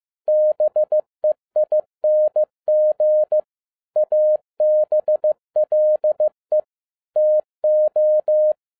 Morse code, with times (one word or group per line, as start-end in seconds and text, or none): 0.3-3.4 BEING
4.0-6.6 ABLE
7.2-8.5 TO